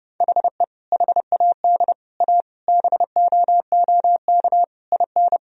Morse code, heard as a keyed tone, about 32 wpm, 700 Hz. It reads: HE HAD A BOOK IN
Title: Sentences from Top 1000 Words - Review Part 3 - 30wpm